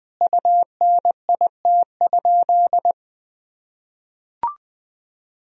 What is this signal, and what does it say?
Morse code, heard as a keyed tone, about 20 wpm, 700 Hz. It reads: UNIT?
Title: Sentences from Top 1000 Words - Review Part 2 - 20wpm